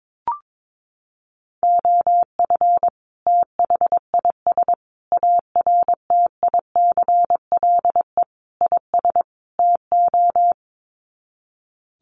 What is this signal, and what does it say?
Morse code, heard as a keyed tone, about 22 wpm, 700 Hz. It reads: OF THIS ARTICLE IS TO
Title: Sets of 5 Words Encore2 22wpm